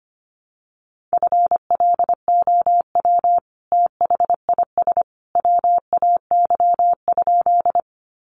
Morse code, heard as a keyed tone, about 25 wpm, 700 Hz. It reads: FLOW THIS WAY?
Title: Sentences from Top 1200 Words 25wpm